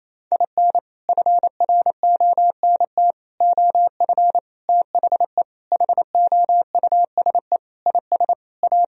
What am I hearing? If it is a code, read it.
Morse code, heard as a keyed tone, about 28 wpm, 700 Hz. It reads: IN FRONT OF THE HOUSE IS A
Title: Sentences from Top 1000 Words - Review Part 4 - 28wpm